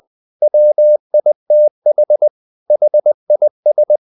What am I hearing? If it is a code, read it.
Morse code, heard as a keyed tone, about 20 wpm, 600 Hz. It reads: WITH HIS
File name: the old man and the sea part 2 20wpm